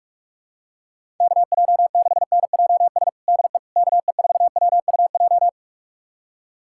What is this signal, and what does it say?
Morse code, heard as a keyed tone, about 46 wpm, 700 Hz. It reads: KJ6NJS DE KE4WUJ
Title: USA Callsign DE Callsign 45wpm